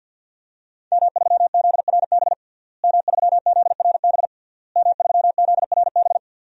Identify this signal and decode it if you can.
Morse code, heard as a keyed tone, about 51 wpm, 700 Hz. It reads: M3ZRB M3ZRB M3ZRB
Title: DXCC-100 Call Signs - 5 Characters - Repeated 3x - 50wpm